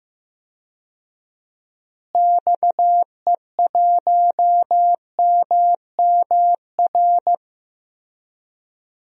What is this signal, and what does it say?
Morse code, heard as a keyed tone, about 15 wpm, 700 Hz. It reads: XE1MMR
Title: DXCC-100 Call Signs - 6 Characters - Repeated 3x - 15wpm